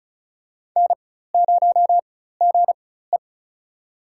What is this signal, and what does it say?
Morse code, heard as a keyed tone, about 36 wpm, spaced as for 16 wpm, 700 Hz. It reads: N0GE
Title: USA Call Signs - 4 Characters - 4x Character Spacing - 35wpm